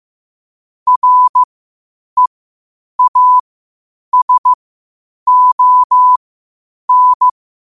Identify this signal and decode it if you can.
Morse code, heard as a keyed tone, about 15 wpm, spaced as for 8 wpm, 1000 Hz. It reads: REASON
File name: qotdfiles / 8wpm